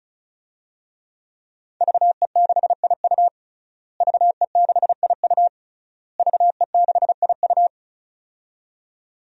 Morse code, heard as a keyed tone, about 36 wpm, 700 Hz. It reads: VE6IU VE6IU VE6IU
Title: DXCC-100 Call Signs - 5 Characters - Repeated 3x - 35wpm